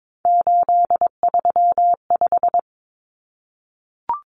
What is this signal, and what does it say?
Morse code, heard as a keyed tone, about 22 wpm, 700 Hz. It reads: 835
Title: Numbers - 3-Digit 22wpm